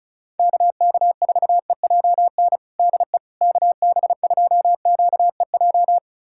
Morse code, heard as a keyed tone, about 36 wpm, 700 Hz. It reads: KK4EJN DE KB2QEJ
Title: USA Callsign DE Callsign 35wpm